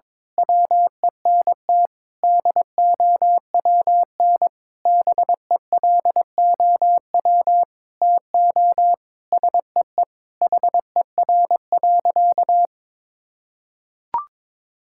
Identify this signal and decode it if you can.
Morse code, heard as a keyed tone, about 22 wpm, 700 Hz. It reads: WENT DOWN BELOW TO SEE HER.
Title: Sentences from Top 1000 Words - Review Part 5 - 22wpm